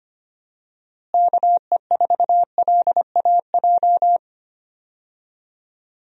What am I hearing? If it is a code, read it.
Morse code, heard as a keyed tone, about 25 wpm, 700 Hz. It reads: KE4LAJ
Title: USA Call Signs - 6 Characters - Repeated 3x - 25wpm